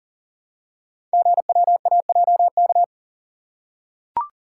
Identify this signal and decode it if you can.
Morse code, heard as a keyed tone, about 38 wpm, 700 Hz. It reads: GWAJK
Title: Code Groups - 5 Letter:Numbers - 40wpm